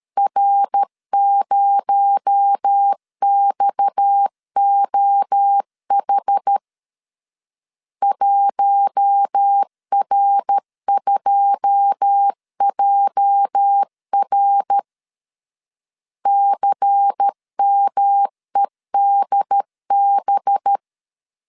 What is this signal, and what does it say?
Morse code, heard as a keyed tone, about 13 wpm, 800 Hz. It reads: R0XOH 1R2JR CMEDB
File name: CW / DET / 1 9 1500C S C 12wpm 3811